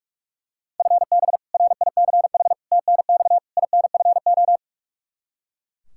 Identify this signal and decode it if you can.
Morse code, heard as a keyed tone, about 42 wpm, 700 Hz. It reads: FB RICH TNX INFO
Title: Practice QSO - Exchange 3 - More Info - 45wpm